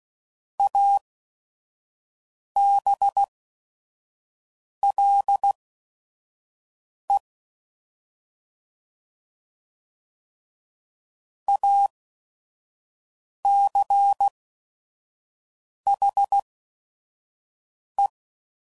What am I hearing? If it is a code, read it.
Morse code, heard as a keyed tone, about 16 wpm, spaced as for 5 wpm, 800 Hz. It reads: ABLE ACHE
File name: Prefixes / By First Letter / As